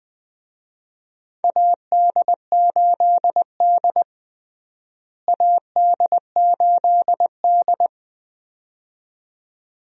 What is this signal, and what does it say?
Morse code, heard as a keyed tone, about 21 wpm, 700 Hz. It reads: AD8D AD8D
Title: USA Call Signs - 4 Characters - Repeated 2x - 20wpm